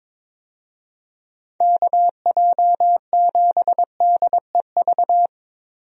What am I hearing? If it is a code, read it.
Morse code, heard as a keyed tone, about 22 wpm, 700 Hz. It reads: KJ7DEV
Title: USA Call Signs - 6 Characters - Repeated 2x - 22wpm